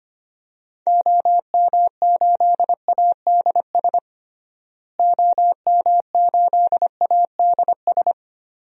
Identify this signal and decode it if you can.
Morse code, heard as a keyed tone, about 25 wpm, 700 Hz. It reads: OM8ADS OM8ADS
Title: DXCC-100 Call Signs - 6 Characters - Repeated 3x - 25wpm